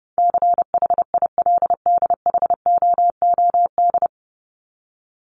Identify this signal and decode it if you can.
Morse code, heard as a keyed tone, about 28 wpm, 700 Hz. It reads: CHILDHOOD
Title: Top 1751-2000 Words 30wpm